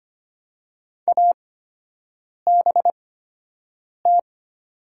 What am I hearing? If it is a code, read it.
Morse code, heard as a keyed tone, about 25 wpm, spaced as for 7 wpm, 700 Hz. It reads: ABT
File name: Top 90 QSO Elements - 8x Character Spacing - 25wpm